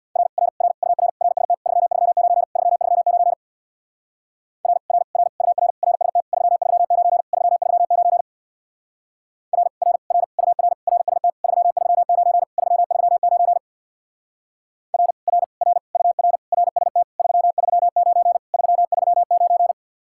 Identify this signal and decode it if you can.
Morse code, heard as a keyed tone, about 61 wpm, 700 Hz. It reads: R R R UR RST 339 339 R R R UR RST 339 339 R R R UR RST 339 339 R R R UR RST 339 339
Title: Speed-Racing Common QSO Phrases - 50wpm